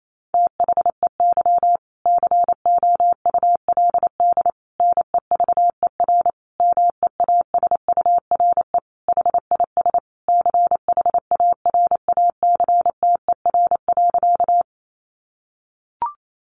Morse code, heard as a keyed tone, about 28 wpm, 700 Hz. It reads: THEY COULD NEVER MEASURE HIS CHARACTER.
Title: Sentences from Top 1200 Words 28wpm